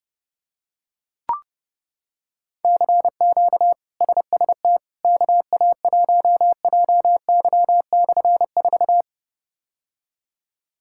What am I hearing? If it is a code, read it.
Morse code, heard as a keyed tone, about 32 wpm, 700 Hz. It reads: CQ SST KA1JY/4
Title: K1USN SST QSO Practice - 30wpm